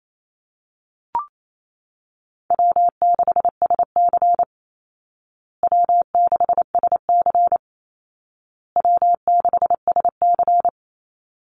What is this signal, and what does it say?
Morse code, heard as a keyed tone, about 28 wpm, 700 Hz. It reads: W6SC W6SC W6SC
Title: Call Signs Repeated 2x - 28wpm